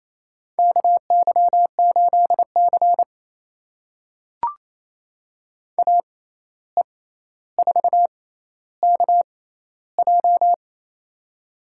Morse code, heard as a keyed tone, about 28 wpm, spaced as for 13 wpm, 700 Hz. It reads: KY8C AE4KJ
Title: USA Call Signs - 6x Character Spacing - 28wpm